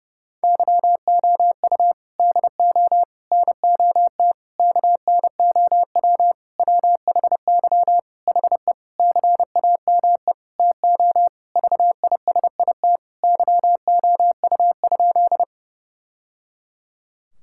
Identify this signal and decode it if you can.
Morse code, heard as a keyed tone, about 32 wpm, 700 Hz. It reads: YOU DO NOT KNOW WHY HE CAME TO VISIT YOU?
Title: Sentences from Top 1000 Words - Review Part 6 - 30wpm